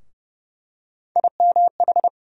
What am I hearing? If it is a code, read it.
Morse code, heard as a keyed tone, about 30 wpm, 700 Hz. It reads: IMH
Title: USA Call Sign Postfixes - 3 Letters 30wpm